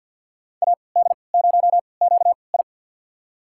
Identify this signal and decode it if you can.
Morse code, heard as a keyed tone, about 47 wpm, spaced as for 28 wpm, 700 Hz. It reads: AD0QI
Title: USA Call Signs - 5 Characters - 3x Character Spacing - 50wpm